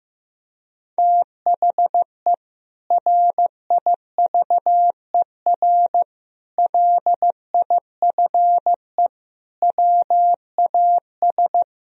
Morse code, heard as a keyed tone, about 16 wpm, 700 Hz. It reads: THE RIVER LIFE WAS
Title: Sentences from Top 1000 Words - Review Part 1 - 15wpm